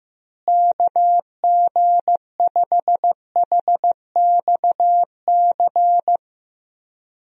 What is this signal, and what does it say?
Morse code, heard as a keyed tone, about 15 wpm, 700 Hz. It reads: KG5HXC
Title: USA Call Signs - 6 Characters - Repeated 3x - 15wpm